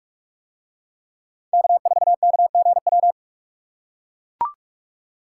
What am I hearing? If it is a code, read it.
Morse code, heard as a keyed tone, about 44 wpm, 700 Hz. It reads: K4KGW